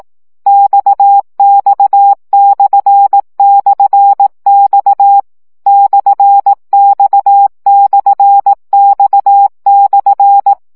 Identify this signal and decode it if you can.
Morse code, heard as a keyed tone, about 18 wpm, 800 Hz. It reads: XX//X /X/X/
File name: CW gyakorlas x per 15 baud